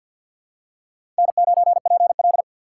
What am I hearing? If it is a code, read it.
Morse code, heard as a keyed tone, about 47 wpm, 700 Hz. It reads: N9PL
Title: USA Call Signs - 4 Characters - 3x Character Spacing - 50wpm